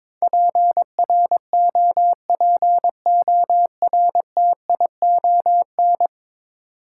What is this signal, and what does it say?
Morse code, heard as a keyed tone, about 22 wpm, 700 Hz. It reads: PROPORTION
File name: practice / 22wpm / Ten Letter Words - 22wpm